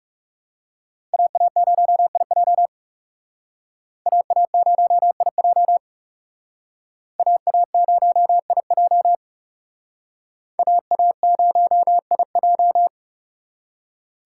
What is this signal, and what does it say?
Morse code, heard as a keyed tone, about 37 wpm, 700 Hz. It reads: AA0IJ AA0IJ AA0IJ AA0IJ
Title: Speed Racing - USA 2x2 Call Signs 30wpm